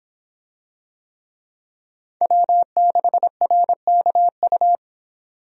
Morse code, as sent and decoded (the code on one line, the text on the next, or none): .-- -.... .-. -.- ..-
W6RKU